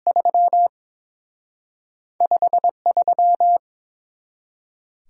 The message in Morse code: 3 53

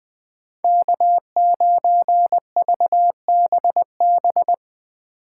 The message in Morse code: K9VBB

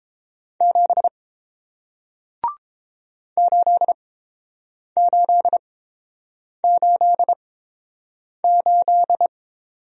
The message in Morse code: 7 8888

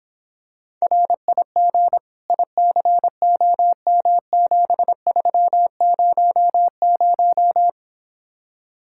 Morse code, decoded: RIG ICOM7300

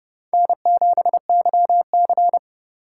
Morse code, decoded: N7YC

